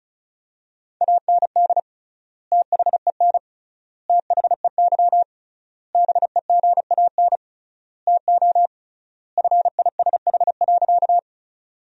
Morse code, heard as a keyed tone, about 36 wpm, 700 Hz. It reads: AND THEN THEY BEGAN TO FISH.
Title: 3x Word-Spacing - Sentences from Top 300 Words - 35wpm